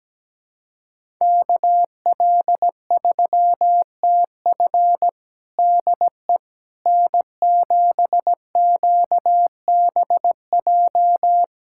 Morse code, heard as a keyed tone, about 17 wpm, 700 Hz. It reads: KL3TF DE N7QBJ
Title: USA Callsign DE Callsign - 5 Characters - 17wpm